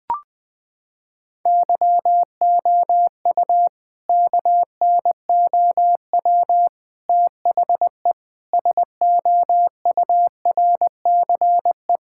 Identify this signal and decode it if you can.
Morse code, heard as a keyed tone, about 20 wpm, 700 Hz. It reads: YOU KNOW THE SOURCE
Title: Sentences from Top 1000 Words - Review Part 5 - 20wpm